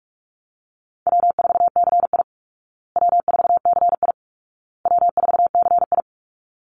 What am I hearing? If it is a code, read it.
Morse code, heard as a keyed tone, about 44 wpm, 700 Hz. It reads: W4CI W4CI W4CI
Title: Call Signs Repeated 2x - 45wpm